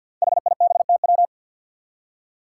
Morse code, heard as a keyed tone, about 47 wpm, 700 Hz. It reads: HIBTW
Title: Five Challenging Letter-Number Combinations 50wpm